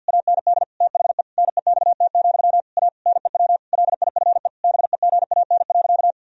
Morse code, heard as a keyed tone, about 47 wpm, 700 Hz. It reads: AND THE NEXT, A NEW LIFE BEGAN.
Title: Sentences from Top 1000 Words - Review Part 1 - 50wpm